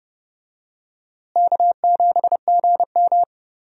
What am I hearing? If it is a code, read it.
Morse code, heard as a keyed tone, about 31 wpm, 700 Hz. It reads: K7GM
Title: USA Call Signs - 4 Characters - Repeated 2x - 30wpm